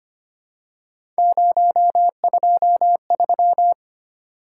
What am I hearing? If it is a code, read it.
Morse code, heard as a keyed tone, about 25 wpm, 700 Hz. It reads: 023
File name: Three Letter-Number Combinations - 25wpm